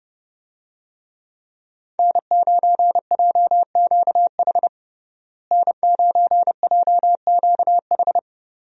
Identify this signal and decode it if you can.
Morse code, heard as a keyed tone, about 30 wpm, 700 Hz. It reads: N9JQH N9JQH
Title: USA Call Signs - 5 Characters - Repeated 2x - 30wpm